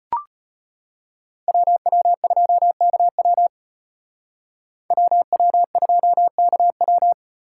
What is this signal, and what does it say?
Morse code, heard as a keyed tone, about 35 wpm, 700 Hz. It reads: WW2KW WW2KW